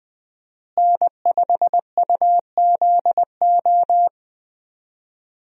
Morse code, heard as a keyed tone, about 20 wpm, 700 Hz. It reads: N5UZO
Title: USA Call Signs - 5 Characters - Repeated 2x - 20wpm